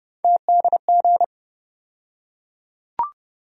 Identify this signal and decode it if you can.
Morse code, heard as a keyed tone, about 30 wpm, 700 Hz. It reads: TDG
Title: USA Call Sign Postfixes - 3 Letters 30wpm